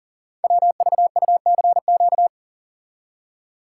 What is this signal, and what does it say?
Morse code, heard as a keyed tone, about 38 wpm, 690 Hz. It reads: WVUCQ
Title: Code Groups - 5 Letters - 40wpm